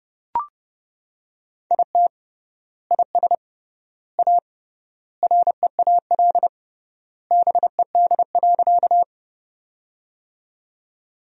Morse code, heard as a keyed tone, about 30 wpm, 700 Hz. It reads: IT IS A REAL BED.
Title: 3x Word-Spacing - Sentences from Top 500 Words - 30wpm